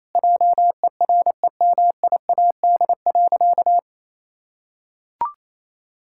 Morse code, heard as a keyed tone, about 28 wpm, 700 Hz. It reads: JEREMIAD.
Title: Pangram Challenge v2 - 28wpm